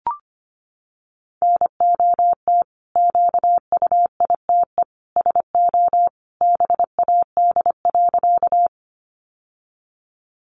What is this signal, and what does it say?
Morse code, heard as a keyed tone, about 25 wpm, 700 Hz. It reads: NOT QUITE SO BAD.
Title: Sentences from Top 700 Words 25wpm